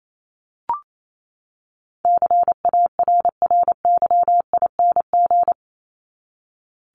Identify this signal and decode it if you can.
Morse code, heard as a keyed tone, about 28 wpm, 700 Hz. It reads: CARRYING
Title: ING Suffix - Top 100 Words with Base Words 28wpm